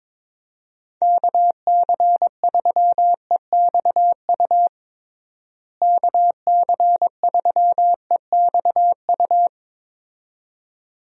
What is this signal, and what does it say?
Morse code, heard as a keyed tone, about 22 wpm, 700 Hz. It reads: KC3EXU KC3EXU
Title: USA Call Signs - 6 Characters - Repeated 2x - 22wpm